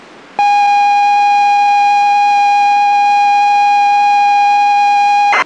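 ZX Spectrum tape loading.
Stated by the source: ZX Spectrum tape loading from the game Deep Zone, dumped from the original cassette.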